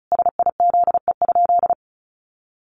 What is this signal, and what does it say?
Morse code, heard as a keyed tone, about 34 wpm, 700 Hz. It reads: SIZE?
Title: Sentences from Top 500 Words 35wpm